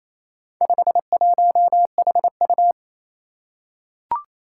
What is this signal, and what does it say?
Morse code, heard as a keyed tone, about 28 wpm, 700 Hz. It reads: 51HU
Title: Four Challenging Letter-Number Combinations 28wpm